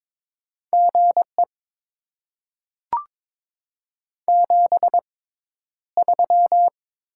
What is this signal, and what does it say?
Morse code, heard as a keyed tone, about 22 wpm, spaced as for 11 wpm, 690 Hz. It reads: GE 73
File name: Top 20 QSO Elements - 6x Character Spacing - 22wpm